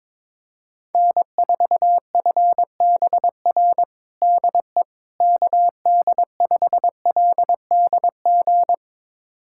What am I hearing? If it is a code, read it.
Morse code, heard as a keyed tone, about 22 wpm, 700 Hz. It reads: N4FBR DE KD5LDG
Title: USA Callsign DE Callsign 22wpm